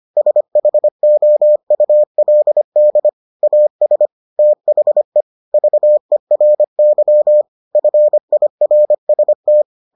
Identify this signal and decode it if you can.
Morse code, heard as a keyed tone, about 25 wpm, 600 Hz. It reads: SHOULD AS THE VERY FIRST